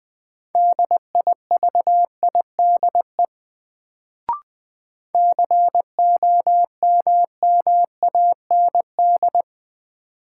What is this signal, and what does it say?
Morse code, heard as a keyed tone, about 21 wpm, 700 Hz. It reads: DIVIDE COMMAND
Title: Sentences from Top 1600 Words - New Word Drill Alt - 20wpm